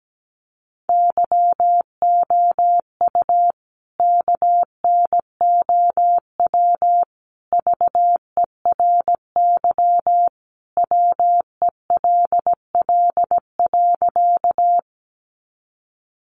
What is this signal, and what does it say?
Morse code, heard as a keyed tone, about 17 wpm, 700 Hz. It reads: YOU KNOW VERY WELL.
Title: A Sound Of Thunder 17wpm